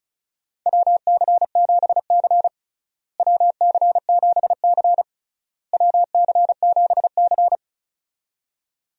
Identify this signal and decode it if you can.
Morse code, heard as a keyed tone, about 34 wpm, 700 Hz. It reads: WC7C WC7C WC7C
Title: USA Call Signs - 4 Characters - Repeated 3x - 35wpm